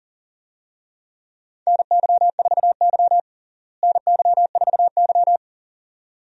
Morse code, heard as a keyed tone, about 37 wpm, 700 Hz. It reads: NY4Y NY4Y
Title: USA Call Signs - 4 Characters - Repeated 2x - 40wpm